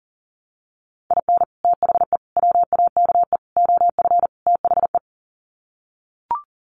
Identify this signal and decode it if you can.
Morse code, heard as a keyed tone, about 40 wpm, 700 Hz. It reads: IN THE WAKE OF THE
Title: Sets of 5 Words 40wpm